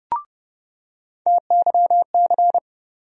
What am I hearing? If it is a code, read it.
Morse code, heard as a keyed tone, about 30 wpm, 700 Hz. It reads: TYC
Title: USA Call Sign Postfixes - 3 Letters 30wpm